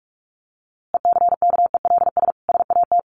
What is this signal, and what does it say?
Morse code, heard as a keyed tone, about 44 wpm, 700 Hz. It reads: ECKELS SAT
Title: A Sound Of Thunder 45wpm